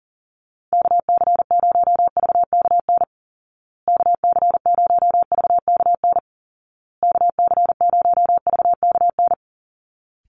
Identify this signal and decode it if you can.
Morse code, heard as a keyed tone, about 40 wpm, 700 Hz. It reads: KC0VKN KC0VKN KC0VKN
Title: Call Signs Repeated 2x - 40wpm